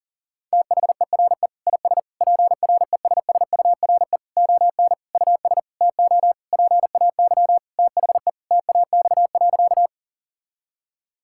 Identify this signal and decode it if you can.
Morse code, heard as a keyed tone, about 40 wpm, 700 Hz. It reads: THERE IS PRESSURE ON US TO PAY THE TAX.
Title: Sentences from Top 1000 Words - Review Part 6 - 40wpm